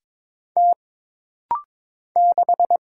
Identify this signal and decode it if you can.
Morse code, heard as a keyed tone, about 22 wpm, 700 Hz. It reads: T 6
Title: Single Letter-Number Alt - Rapid-Fire 22wpm